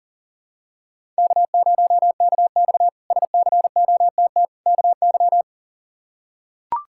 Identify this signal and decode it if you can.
Morse code, heard as a keyed tone, about 40 wpm, 700 Hz. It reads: K0KX SCOTT KY
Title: K1USN SST QSO Practice - 40wpm